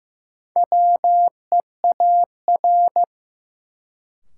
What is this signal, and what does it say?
Morse code, heard as a keyed tone, about 15 wpm, 700 Hz. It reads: WEAR
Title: Sentences from Top 1500 Words - New Word Drill Alt - 15wpm